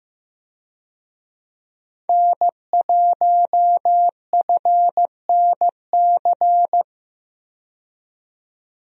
Morse code, heard as a keyed tone, about 15 wpm, 700 Hz. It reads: N1FNC
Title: USA Call Signs - 5 Characters - 15wpm